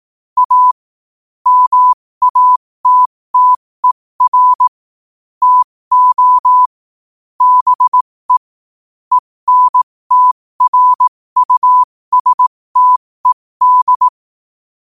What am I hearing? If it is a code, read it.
Morse code, heard as a keyed tone, about 18 wpm, spaced as for 15 wpm, 1000 Hz. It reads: A MATTER TO BE ENTRUSTED
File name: qotdfiles / 15wpm